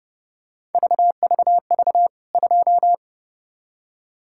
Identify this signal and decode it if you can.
Morse code, heard as a keyed tone, about 30 wpm, 700 Hz. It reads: VVV 2
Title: K1USN SST QSO Practice 30wpm